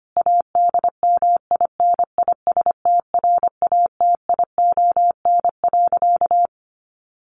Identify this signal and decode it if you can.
Morse code, heard as a keyed tone, about 25 wpm, 700 Hz. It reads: ADMINISTRATION.
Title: Sentences from Top 1300 Words 25wpm